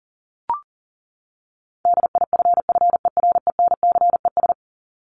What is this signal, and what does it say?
Morse code, heard as a keyed tone, about 40 wpm, 700 Hz. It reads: DIFFERENCES
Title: ES:S Suffix - Top 100 Words with Base Words 40wpm